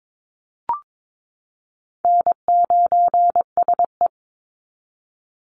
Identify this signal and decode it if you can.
Morse code, heard as a keyed tone, about 22 wpm, 700 Hz. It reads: N9SE
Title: Easy Call Signs Repeated 2x - 22wpm